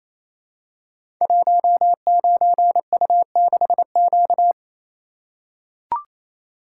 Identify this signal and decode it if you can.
Morse code, heard as a keyed tone, about 29 wpm, 700 Hz. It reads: 19U6Q
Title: Code Groups - 5 Letter:Numbers - 28wpm